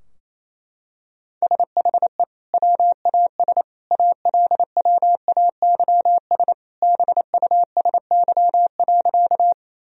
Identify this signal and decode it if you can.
Morse code, heard as a keyed tone, about 28 wpm, 700 Hz. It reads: SHE WAS ALWAYS BUSY.
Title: Sentences from Top 1000 Words - Review Part 3 - 28wpm